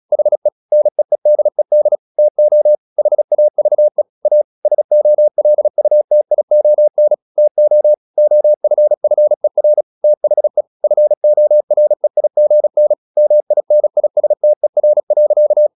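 Morse code, heard as a keyed tone, about 36 wpm, 600 Hz. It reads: HE NEEDED TO HAVE A SOLUTION TO OFFER THE FOREIGN MINISTER.
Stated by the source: Morse practice at 36 wpm, text of Ken Follett's novel Never, from CW med Gnister.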